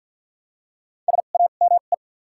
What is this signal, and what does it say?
Morse code, heard as a keyed tone, about 48 wpm, spaced as for 35 wpm, 700 Hz. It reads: SAME
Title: 2x Character Spacing - Top 200 Words - 50wpm